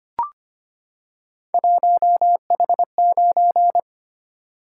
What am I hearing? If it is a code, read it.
Morse code, heard as a keyed tone, about 26 wpm, 700 Hz. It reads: 1H9
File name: Three Letter-Number Combinations - 25wpm